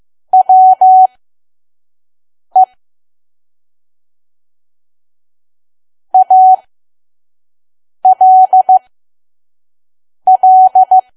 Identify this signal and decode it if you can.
Morse code, heard as a keyed tone, about 15 wpm, spaced as for 5 wpm, 750 Hz. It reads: WE ALL
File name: Archive / 5 WPM files / PM